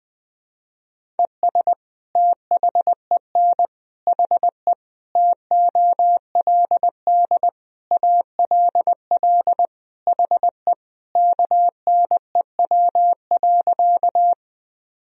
Morse code, heard as a keyed tone, about 20 wpm, 700 Hz. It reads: ES THEN HE TOLD ALL HE KNEW.
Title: Sentences from Top 400 Words and Abbreviations 20wpm